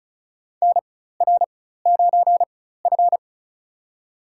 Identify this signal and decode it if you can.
Morse code, heard as a keyed tone, about 36 wpm, spaced as for 16 wpm, 700 Hz. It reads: NR9F
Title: USA Call Signs - 4 Characters - 4x Character Spacing - 35wpm